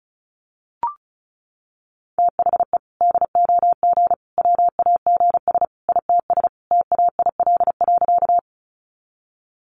Morse code, heard as a keyed tone, about 34 wpm, 700 Hz. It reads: THE DOG WAGS ITS TAIL.
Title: Easy Phrases 35wpm